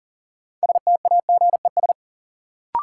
Morse code, heard as a keyed tone, about 37 wpm, 700 Hz. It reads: STAGES
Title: Sets of 5 Words Encore - New Word Drill - 40wpm